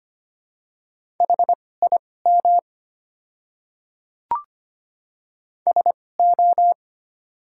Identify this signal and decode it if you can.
Morse code, heard as a keyed tone, about 25 wpm, spaced as for 18 wpm, 690 Hz. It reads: HIM SO
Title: 2x Character Spacing - Top 100 Words - 25wpm